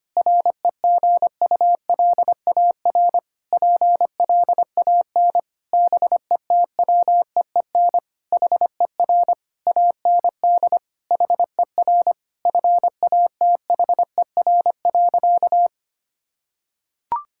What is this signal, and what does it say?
Morse code, heard as a keyed tone, about 25 wpm, 700 Hz. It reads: REGULAR PLAN BETWEEN HER AND HER FATHER.
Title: Sentences from Top 1000 Words - Review Part 6 - 25wpm